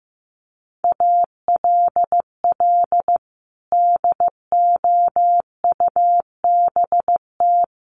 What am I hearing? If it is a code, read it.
Morse code, heard as a keyed tone, about 15 wpm, 700 Hz. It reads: ALL DOUBT
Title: Sentences from Top 1500 Words 15wpm